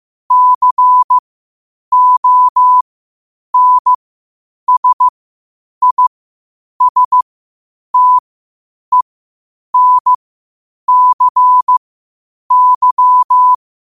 Morse code, heard as a keyed tone, about 15 wpm, spaced as for 8 wpm, 1000 Hz. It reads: CONSISTENCY
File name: qotdfiles / 8wpm